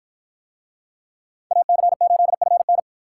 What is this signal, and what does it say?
Morse code, heard as a keyed tone, about 50 wpm, 700 Hz. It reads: AC8FN